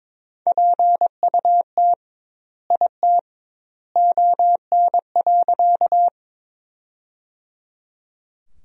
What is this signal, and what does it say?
Morse code, heard as a keyed tone, about 22 wpm, 700 Hz. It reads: PUT IT ON.
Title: 2x Word-Spacing - Sentences from Top 200 Words - 22wpm